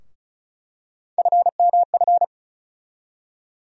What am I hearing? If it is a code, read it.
Morse code, heard as a keyed tone, about 34 wpm, 700 Hz. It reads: FMF
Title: USA Call Sign Postfixes - 3 Letters 35wpm